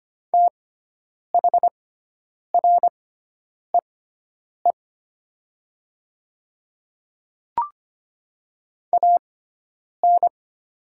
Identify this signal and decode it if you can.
Morse code, heard as a keyed tone, about 25 wpm, spaced as for 9 wpm, 700 Hz. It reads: THREE AN
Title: 6x Character Spacing - Top 200 Words - 25wpm